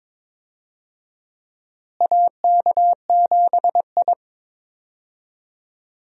Morse code, heard as a keyed tone, about 22 wpm, 700 Hz. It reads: AK7I
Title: USA Call Signs - 4 Characters - Repeated 2x - 22wpm